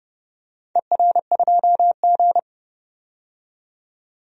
Morse code, heard as a keyed tone, about 30 wpm, 700 Hz. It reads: ER2G
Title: DXCC-100 Call Signs - 4 Characters - 30wpm